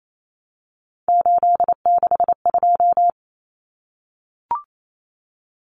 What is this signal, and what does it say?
Morse code, heard as a keyed tone, about 28 wpm, 700 Hz. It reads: 862
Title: Numbers - 3-Digit 28wpm